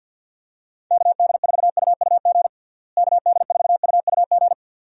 Morse code, heard as a keyed tone, about 51 wpm, 700 Hz. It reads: KD4UUG KD4UUG
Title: USA Call Signs - 6 Characters - Repeated 2x - 50wpm